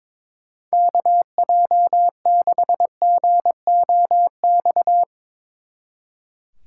Morse code, heard as a keyed tone, about 24 wpm, 700 Hz. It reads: KJ6GOX
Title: Speed Racing - USA 2x3 Call Signs 22wpm